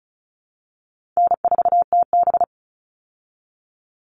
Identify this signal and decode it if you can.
Morse code, heard as a keyed tone, about 34 wpm, 700 Hz. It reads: N4TB
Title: Easy Call Signs Repeated 2x - 35wpm